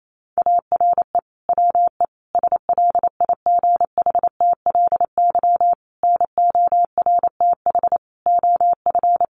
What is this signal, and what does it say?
Morse code, heard as a keyed tone, about 28 wpm, 700 Hz. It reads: ARE WE SLIGHTLY NORTH OF
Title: Sentences from Top 1300 Words 28wpm